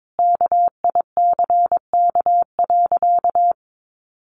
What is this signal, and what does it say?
Morse code, heard as a keyed tone, about 22 wpm, 700 Hz. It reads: KICK.
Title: Sentences from Top 1300 Words 22wpm